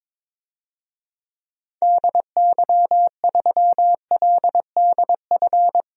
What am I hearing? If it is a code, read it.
Morse code, heard as a keyed tone, about 22 wpm, 700 Hz. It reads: DY3LDF
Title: DXCC-100 Call Signs - 6 Characters - 22wpm